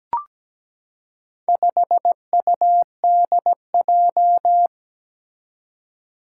Morse code, heard as a keyed tone, about 17 wpm, 700 Hz. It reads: 5UDJ
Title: Four Challenging Letter-Number Combinations 17wpm